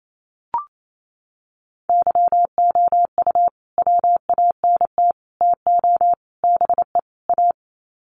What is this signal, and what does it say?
Morse code, heard as a keyed tone, about 28 wpm, 700 Hz. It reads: YOU WANT TO BE A
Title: Sets of 5 Words Encore2 28wpm